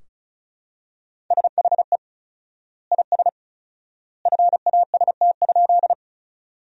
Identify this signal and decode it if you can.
Morse code, heard as a keyed tone, about 34 wpm, 700 Hz. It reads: SHE IS FAST?
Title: 4x Word-Spacing - Sentences from Top 400 Words - 35wpm